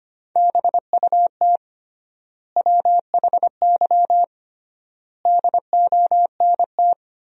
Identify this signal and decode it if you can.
Morse code, heard as a keyed tone, about 25 wpm, 700 Hz. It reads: BUT WHY DONT
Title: 3x Word-Spacing - Sentences from Top 300 Words - 25wpm